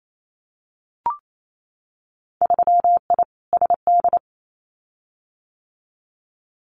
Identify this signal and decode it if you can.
Morse code, heard as a keyed tone, about 28 wpm, 700 Hz. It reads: 3I SD
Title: Winter Field Day Exchanges Encore 28wpm